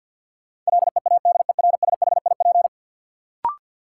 Morse code, heard as a keyed tone, about 53 wpm, 700 Hz. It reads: LEADERSHIP
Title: Sentences from Top 1600 Words - New Word Drill - 50wpm